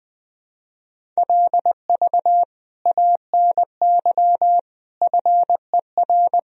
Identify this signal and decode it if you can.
Morse code, heard as a keyed tone, about 21 wpm, 700 Hz. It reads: LV ANY FER
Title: Sentences from Top 400 Words and Abbreviations 20wpm